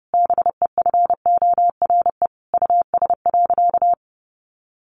Morse code, heard as a keyed tone, about 28 wpm, 700 Hz. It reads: BEFORE US.
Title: Sentences from Top 600 Words 30wpm